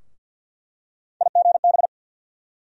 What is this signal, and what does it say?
Morse code, heard as a keyed tone, about 47 wpm, 690 Hz. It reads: IGB
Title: USA Call Sign Postfixes - 3 Letters 50wpm